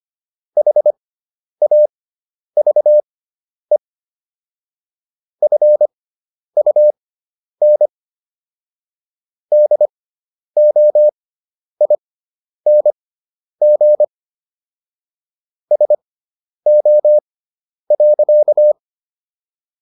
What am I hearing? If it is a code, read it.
Morse code, heard as a keyed tone, about 25 wpm, spaced as for 10 wpm, 600 Hz. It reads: HAVE FUN DOING SO.